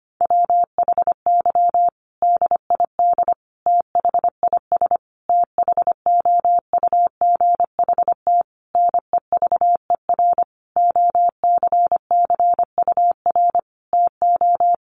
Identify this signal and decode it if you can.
Morse code, heard as a keyed tone, about 25 wpm, 700 Hz. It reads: WHY DID THIS THOUGHT NEVER OCCUR TO
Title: Sentences from Top 900 Words 25wpm